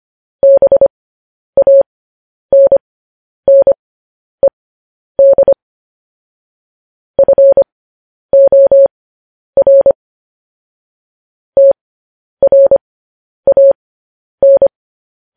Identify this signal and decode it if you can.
Morse code, heard as a keyed tone, about 25 wpm, spaced as for 10 wpm, 550 Hz. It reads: BANNED FOR TRAN
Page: news-headlines-in-morse-code-at-10-wpm